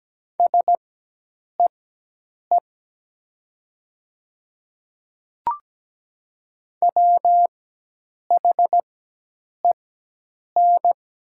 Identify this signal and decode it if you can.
Morse code, heard as a keyed tone, about 17 wpm, spaced as for 8 wpm, 700 Hz. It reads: SEE WHEN
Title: 4x Character Spacing - Top 200 Words - 17wpm